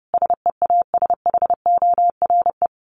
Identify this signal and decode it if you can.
Morse code, heard as a keyed tone, about 30 wpm, 700 Hz. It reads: SEASHORE
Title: Dit Challenge 30wpm